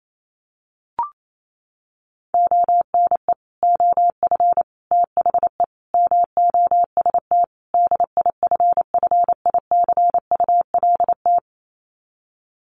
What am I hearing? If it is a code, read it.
Morse code, heard as a keyed tone, about 28 wpm, 700 Hz. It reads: ONE OF THE MOST DIFFICULT
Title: Sets of 5 Words Encore2 28wpm